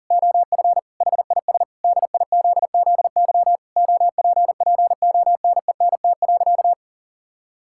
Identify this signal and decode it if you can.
Morse code, heard as a keyed tone, about 40 wpm, 700 Hz. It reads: OF HIS DIZZY OPPONENT.
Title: Pangram Challenge v2 - 40wpm